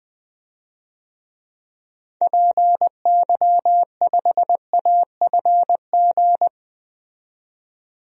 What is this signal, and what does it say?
Morse code, heard as a keyed tone, about 20 wpm, 700 Hz. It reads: PY5AFG
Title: DXCC-100 Call Signs - 6 Characters - 20wpm